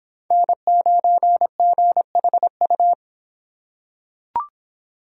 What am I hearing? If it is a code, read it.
Morse code, heard as a keyed tone, about 26 wpm, 700 Hz. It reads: N9GHU